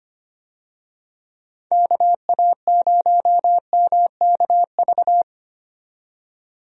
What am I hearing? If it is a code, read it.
Morse code, heard as a keyed tone, about 25 wpm, 700 Hz. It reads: KA0MKV
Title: USA Call Signs - 6 Characters - 25wpm